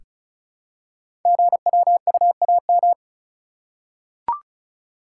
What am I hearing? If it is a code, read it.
Morse code, heard as a keyed tone, about 36 wpm, 700 Hz. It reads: GWUAM
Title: Code Groups - 5 Letters - 35wpm